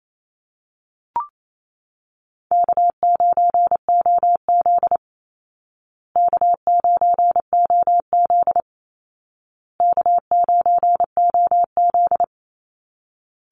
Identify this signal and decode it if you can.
Morse code, heard as a keyed tone, about 28 wpm, 700 Hz. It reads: K9OZ K9OZ K9OZ
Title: Easy Call Signs Repeated 3x - 28wpm